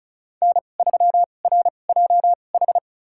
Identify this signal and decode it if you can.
Morse code, heard as a keyed tone, about 36 wpm, spaced as for 25 wpm, 700 Hz. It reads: N3RJH
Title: USA Call Signs - 5 Characters - 2x Character Spacing - 35wpm